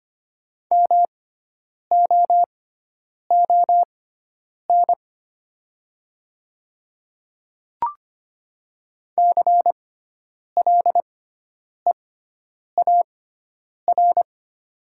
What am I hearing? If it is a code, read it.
Morse code, heard as a keyed tone, about 25 wpm, spaced as for 9 wpm, 700 Hz. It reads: MOON CLEAR
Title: 6x Character Spacing - Top 500 Words - 25wpm